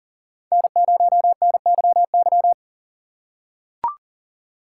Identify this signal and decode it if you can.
Morse code, heard as a keyed tone, about 38 wpm, 700 Hz. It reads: N0NYY
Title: USA Call Signs - 5 Characters - 4x Character Spacing - 40wpm